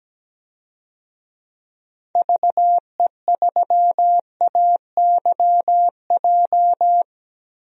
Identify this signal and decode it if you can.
Morse code, heard as a keyed tone, about 17 wpm, 700 Hz. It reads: VE3AYJ
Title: DXCC-100 Call Signs - 6 Characters - 17wpm